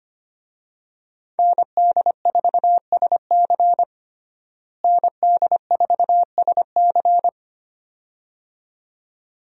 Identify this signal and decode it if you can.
Morse code, heard as a keyed tone, about 25 wpm, 700 Hz. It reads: ND4SC ND4SC
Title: USA Call Signs - 5 Characters - Repeated 2x - 25wpm